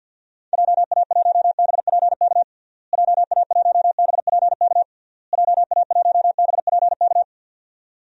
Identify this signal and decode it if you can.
Morse code, heard as a keyed tone, about 50 wpm, 700 Hz. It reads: JA1BPK JA1BPK JA1BPK
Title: DXCC-100 Call Signs - 6 Characters - Repeated 3x - 50wpm